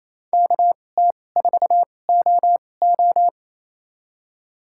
Morse code, heard as a keyed tone, about 28 wpm, spaced as for 20 wpm, 700 Hz. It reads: KT4OO
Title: USA Call Signs - 2x Character Spacing - 28wpm